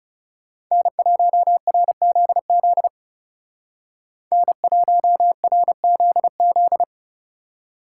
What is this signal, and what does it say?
Morse code, heard as a keyed tone, about 39 wpm, 700 Hz. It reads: N1RZZ N1RZZ